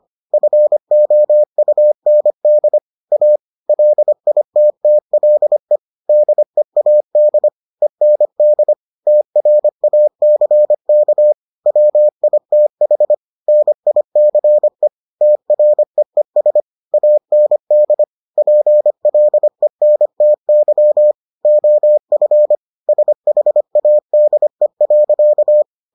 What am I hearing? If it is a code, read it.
Morse code, heard as a keyed tone, about 25 wpm, 600 Hz. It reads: FOUND A LITTLE DEAD END TRACK WITH NICE TREES AND PLENTY OF SHADE.